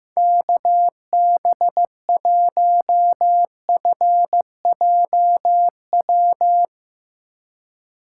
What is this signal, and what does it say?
Morse code, heard as a keyed tone, about 15 wpm, 700 Hz. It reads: KB1FJW
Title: USA Call Signs - 6 Characters - 2x Character Spacing - 15wpm